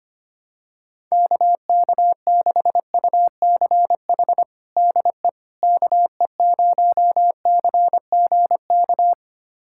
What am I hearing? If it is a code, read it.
Morse code, heard as a keyed tone, about 25 wpm, 700 Hz. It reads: KK6UCH DE KE0CGK
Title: USA Callsign DE Callsign 25wpm